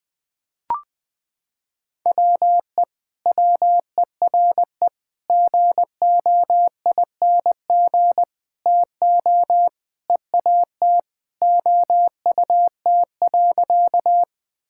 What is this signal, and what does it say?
Morse code, heard as a keyed tone, about 20 wpm, 700 Hz. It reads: WE WERE GOING TO EAT OUT.
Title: Sentences with 4-Word Phrases - 20wpm